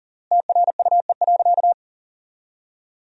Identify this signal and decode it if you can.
Morse code, heard as a keyed tone, about 40 wpm, 700 Hz. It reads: TRUE.
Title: 3x Word-Spacing - Sentences from Top 400 Words - 40wpm